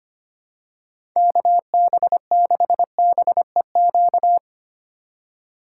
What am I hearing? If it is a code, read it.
Morse code, heard as a keyed tone, about 25 wpm, 700 Hz. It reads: KB6BEQ
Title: USA Call Signs - 6 Characters - 3x Character Spacing - 25wpm